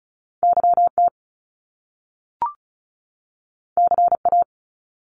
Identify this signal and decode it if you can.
Morse code, heard as a keyed tone, about 36 wpm, 700 Hz. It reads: YT CA
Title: Top Two Letter Combinations in Words 35wpm